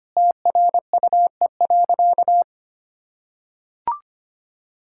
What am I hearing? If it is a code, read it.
Morse code, heard as a keyed tone, about 25 wpm, 700 Hz. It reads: TRUE.
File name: practice / 25wpm / Sentences with 4-Word Phrases - 25wpm